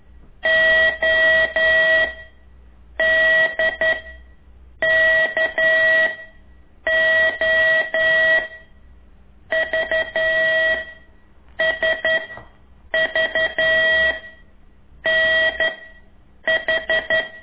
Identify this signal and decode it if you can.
Morse code, heard as a keyed tone, about 11 wpm, 1950 Hz. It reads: O D K O V S V N H
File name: ik1weg2 / CORSI CW